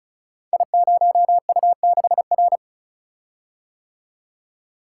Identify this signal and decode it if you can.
Morse code, heard as a keyed tone, about 36 wpm, 690 Hz. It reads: I0U6R
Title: Code Groups - 5 Letter:Numbers - 35wpm